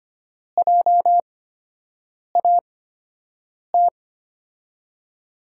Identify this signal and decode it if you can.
Morse code, heard as a keyed tone, about 25 wpm, spaced as for 7 wpm, 700 Hz. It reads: JAT
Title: USA Call Signs - 6 Characters - 8x Character Spacing - 25wpm